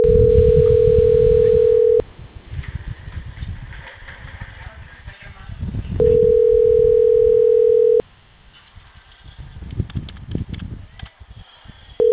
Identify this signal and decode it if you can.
Ringback tone (North America), 440 + 480 Hz.